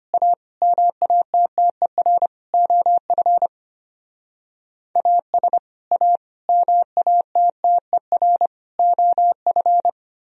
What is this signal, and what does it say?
Morse code, heard as a keyed tone, about 31 wpm, 700 Hz. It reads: A MATTER OF AS A MATTER OF
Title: Speed-Racing - Sets of 4 Words 25wpm